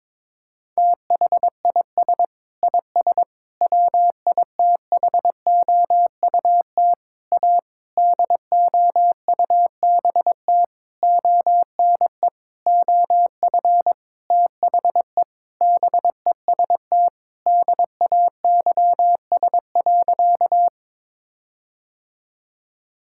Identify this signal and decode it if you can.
Morse code, heard as a keyed tone, about 22 wpm, 700 Hz. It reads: THIS IS WITHOUT A DOUBT ONE OF THE BEST DAYS.
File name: Sentences with 7-Word Phrases - 22wpm